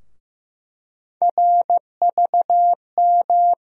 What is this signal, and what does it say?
Morse code, heard as a keyed tone, about 15 wpm, 700 Hz. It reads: RVM
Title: USA Call Sign Postfixes - 3 Letters 15wpm